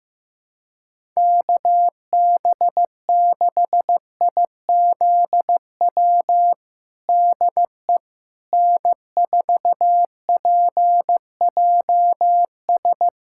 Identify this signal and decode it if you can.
Morse code, heard as a keyed tone, about 15 wpm, 700 Hz. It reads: KB6IZW DE N4PJS
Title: USA Callsign DE Callsign 15wpm